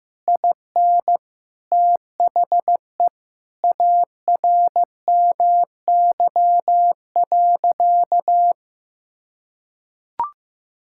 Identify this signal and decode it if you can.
Morse code, heard as a keyed tone, about 15 wpm, 700 Hz. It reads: IN THE ARMY.
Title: Sentences from Top 1000 Words - Review Part 6 - 15wpm